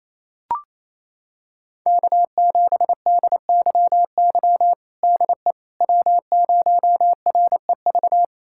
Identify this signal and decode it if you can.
Morse code, heard as a keyed tone, about 28 wpm, 700 Hz. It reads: K7DYY DE W0REV
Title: USA Callsign DE Callsign - 5 Characters - 28wpm